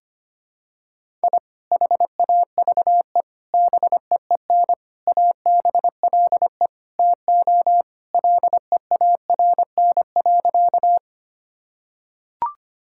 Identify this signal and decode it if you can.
Morse code, heard as a keyed tone, about 25 wpm, 700 Hz. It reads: I HAVE BEEN ABLE TO LEARN.
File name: practice / 25wpm / Sentences with 3-Word Phrases - 25wpm